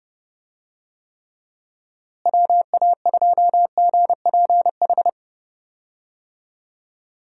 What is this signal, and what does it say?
Morse code, heard as a keyed tone, about 30 wpm, 700 Hz. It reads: WA2GPH
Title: USA Call Signs - 6 Characters - Repeated 3x - 30wpm